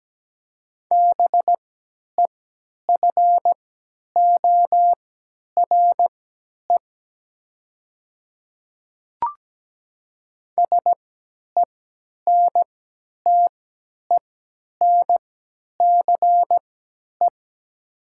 Morse code, heard as a keyed tone, about 17 wpm, spaced as for 10 wpm, 700 Hz. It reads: BEFORE SENTENCE
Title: 3x Character Spacing - Top 200 Words - 17wpm